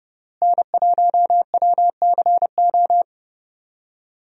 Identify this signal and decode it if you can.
Morse code, heard as a keyed tone, about 30 wpm, 700 Hz. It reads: N1WCO
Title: USA Call Signs - 5 Characters - Repeated 3x - 30wpm